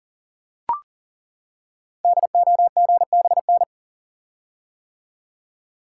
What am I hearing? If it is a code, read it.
Morse code, heard as a keyed tone, about 40 wpm, 690 Hz. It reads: DOGBN
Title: Code Groups - 5 Letters - 40wpm